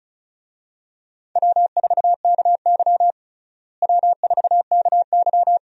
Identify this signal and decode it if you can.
Morse code, heard as a keyed tone, about 36 wpm, 700 Hz. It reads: W4KY W4KY
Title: USA Call Signs - 4 Characters - Repeated 2x - 35wpm